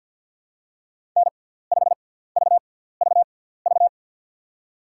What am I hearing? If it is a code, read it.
Morse code, heard as a keyed tone, about 53 wpm, spaced as for 17 wpm, 700 Hz. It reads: N5VVV
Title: USA Call Signs - 5 Characters - 6x Character Spacing - 50wpm